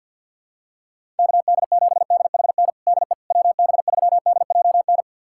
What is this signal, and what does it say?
Morse code, heard as a keyed tone, about 52 wpm, 700 Hz. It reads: KD7DHN DE WB3DJN
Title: USA Callsign DE Callsign 50wpm